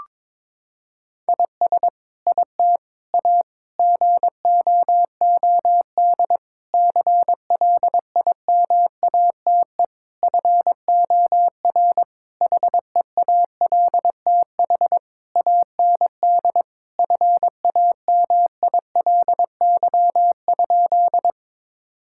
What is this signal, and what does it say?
Morse code, heard as a keyed tone, about 22 wpm, 700 Hz. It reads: IS IT A GOOD CLIMATE FOR HEALTH AND FAMILY?
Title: Sentences from Top 1600 Words 22wpm